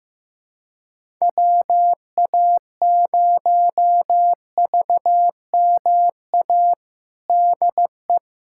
Morse code, heard as a keyed tone, about 15 wpm, 700 Hz. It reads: WA0VMA DE
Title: USA Callsign DE Callsign 15wpm